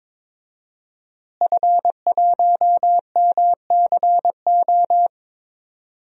Morse code, heard as a keyed tone, about 21 wpm, 700 Hz. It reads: F1MCO
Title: DXCC-100 Call Signs - 5 Characters - Repeated 2x - 22wpm